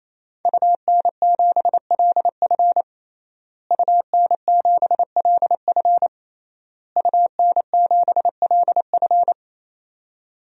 Morse code, heard as a keyed tone, about 28 wpm, 690 Hz. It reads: UN7LF UN7LF UN7LF
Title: DXCC-100 Call Signs - 5 Characters - Repeated 3x - 28wpm